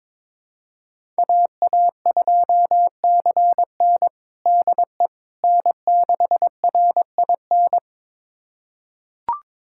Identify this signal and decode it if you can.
Morse code, heard as a keyed tone, about 21 wpm, 700 Hz. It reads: AA2CN DE N6RIN
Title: USA Callsign DE Callsign - 5 Characters - 22wpm